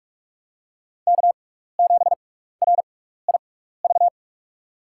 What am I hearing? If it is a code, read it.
Morse code, heard as a keyed tone, about 45 wpm, spaced as for 15 wpm, 700 Hz. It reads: K7RIV
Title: USA Call Signs - 6x Character Spacing - 45wpm